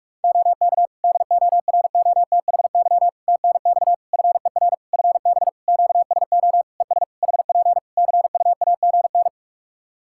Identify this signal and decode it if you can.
Morse code, heard as a keyed tone, about 46 wpm, 700 Hz. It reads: OK DOROTHY TNX FER FB QSO ES HP CUAGN
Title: Practice QSO - Exchange 4 - Ending - 45wpm